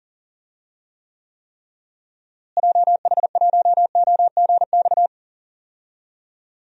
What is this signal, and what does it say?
Morse code, heard as a keyed tone, about 40 wpm, 700 Hz. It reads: JH1OGX
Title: DXCC-100 Call Signs - 6 Characters - Repeated 3x - 40wpm